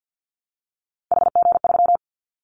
Morse code, heard as a keyed tone, about 50 wpm, 700 Hz. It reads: HG3
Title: Callsign Prefixes for Top 50 DXCC - 50wpm